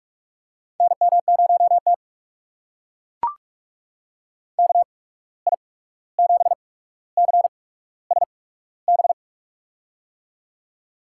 Morse code, heard as a keyed tone, about 43 wpm, spaced as for 17 wpm, 700 Hz. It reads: NM0T KI7CSB
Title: USA Call Signs - 8x Character Spacing - 45wpm